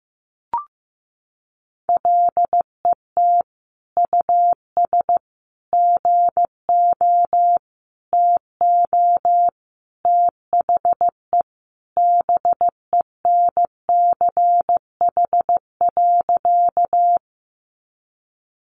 Morse code, heard as a keyed tone, about 15 wpm, 700 Hz. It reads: LET US GO TO THE BENCH.
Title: Sentences from Top 1400 Words 15wpm